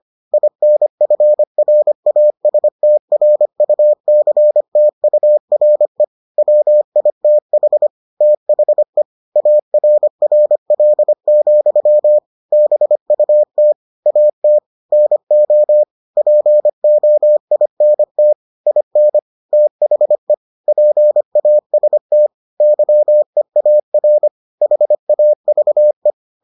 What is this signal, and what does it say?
Morse code, heard as a keyed tone, about 25 wpm, 600 Hz. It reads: INFRASTRUCTURE WITH THE ARRL, BUT AT NO POINT IN THE PAST YEAR HAVE